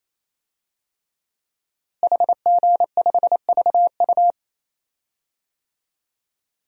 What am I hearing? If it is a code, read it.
Morse code, heard as a keyed tone, about 28 wpm, 700 Hz. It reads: HG5VU
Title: DXCC-100 Call Signs - 5 Characters - Repeated 3x - 28wpm